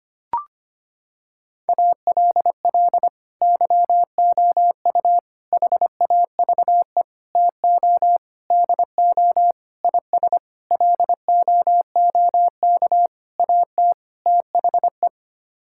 Morse code, heard as a keyed tone, about 25 wpm, 700 Hz. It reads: ALL YOU HAVE TO DO IS LOOK AT THE
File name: Sets of 9 Words - 25wpm